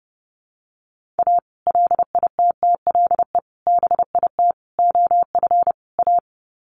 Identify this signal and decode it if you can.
Morse code, heard as a keyed tone, about 28 wpm, 700 Hz. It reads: A LITTLE BIT OF A
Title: Sets of 5 Words 30wpm